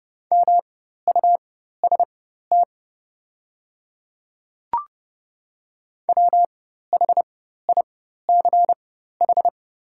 Morse code, heard as a keyed tone, about 30 wpm, spaced as for 14 wpm, 700 Hz. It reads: MUST WHICH
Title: 4x Character Spacing - Top 200 Words - 30wpm